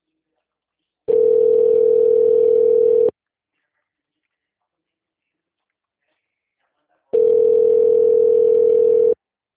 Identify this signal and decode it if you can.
Ringback tone (North America), 440 + 480 Hz.